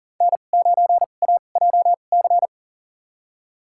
Morse code, heard as a keyed tone, about 40 wpm, spaced as for 29 wpm, 700 Hz. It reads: N9AJC
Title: USA Call Signs - 5 Characters - 2x Character Spacing - 40wpm